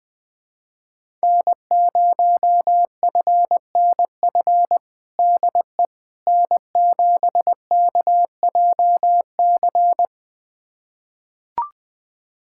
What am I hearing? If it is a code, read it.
Morse code, heard as a keyed tone, about 21 wpm, 700 Hz. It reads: N0FNF DE N7KJC
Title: USA Callsign DE Callsign - 5 Characters - 20wpm